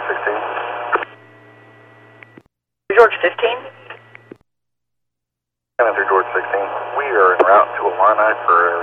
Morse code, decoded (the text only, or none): none